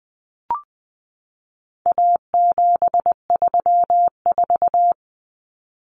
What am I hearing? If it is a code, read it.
Morse code, heard as a keyed tone, about 20 wpm, 700 Hz. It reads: A734
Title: Common Cut Numbers - 4-Digit 20wpm